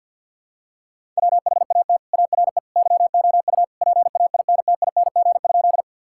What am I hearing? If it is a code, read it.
Morse code, heard as a keyed tone, about 50 wpm, 700 Hz. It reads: WHAT ARE YOU PAINTING?
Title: Sentences from Top 1000 Words - Review Part 6 - 50wpm